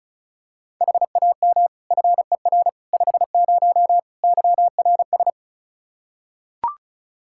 Morse code, heard as a keyed tone, about 34 wpm, 700 Hz. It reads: HAM FER 50 YRS
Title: Practice QSO - Exchange 3 - More Info - 35wpm